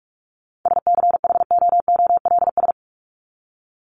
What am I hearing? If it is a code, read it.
Morse code, heard as a keyed tone, about 44 wpm, 700 Hz. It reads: SCHOOLS
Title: ES:S Suffix - Top 100 Words with Base Words 45wpm